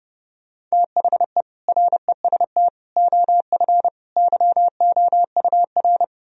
Morse code, heard as a keyed tone, about 40 wpm, 700 Hz. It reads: THE REST OF YOUR